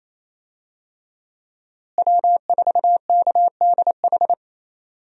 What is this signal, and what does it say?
Morse code, heard as a keyed tone, about 28 wpm, 700 Hz. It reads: W4KDH
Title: USA Call Signs - 5 Characters - Repeated 2x - 28wpm